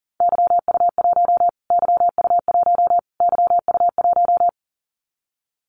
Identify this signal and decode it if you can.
Morse code, heard as a keyed tone, about 40 wpm, 700 Hz. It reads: YU1 YU1 YU1
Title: Callsign Prefixes for Top 25 DXCC - Repeated 3x - 40wpm